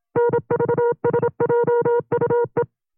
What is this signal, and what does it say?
Morse code, heard as a keyed tone, about 27 wpm, 480 Hz. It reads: NVSJUE
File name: CW-practice/25WPM